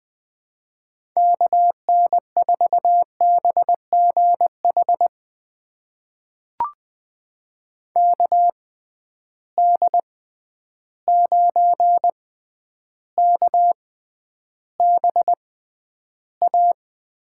Morse code, heard as a keyed tone, about 20 wpm, spaced as for 10 wpm, 700 Hz. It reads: KN4BGH KD9KBA
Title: USA Call Signs - 6 Characters - 6x Character Spacing - 20wpm